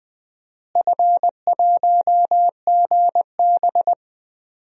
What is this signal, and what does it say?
Morse code, heard as a keyed tone, about 20 wpm, 700 Hz. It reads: F1GB
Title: DXCC-100 Call Signs - 4 Characters - Repeated 2x - 20wpm